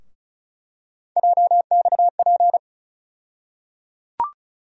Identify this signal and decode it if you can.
Morse code, heard as a keyed tone, about 36 wpm, 700 Hz. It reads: JXP
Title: USA Call Sign Postfixes - 3 Letters 35wpm